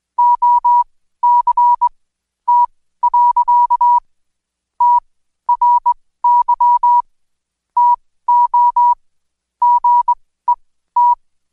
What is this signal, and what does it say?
Morse code, heard as a keyed tone, about 21 wpm, spaced as for 14 wpm, 970 Hz. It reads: OCT. TRY TO GET
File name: Code Quick / AudioDownld2 / Seg124-Day29-DayDrill 29